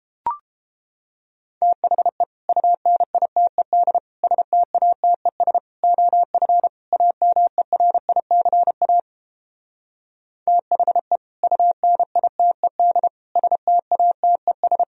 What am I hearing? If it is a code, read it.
Morse code, heard as a keyed tone, about 29 wpm, 700 Hz. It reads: THE UNITED STATES OF AMERICA THE UNITED STATES